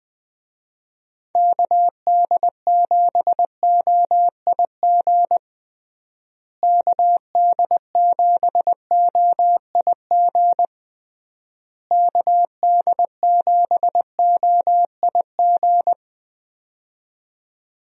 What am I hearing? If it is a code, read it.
Morse code, heard as a keyed tone, about 20 wpm, 700 Hz. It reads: KD7OIG KD7OIG KD7OIG
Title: USA Call Signs - 6 Characters - Repeated 3x - 20wpm